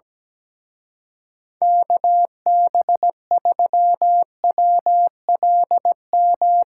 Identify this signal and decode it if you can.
Morse code, heard as a keyed tone, about 18 wpm, 700 Hz. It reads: KB3WLM
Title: Speed Racing - USA 2x3 Call Signs 17wpm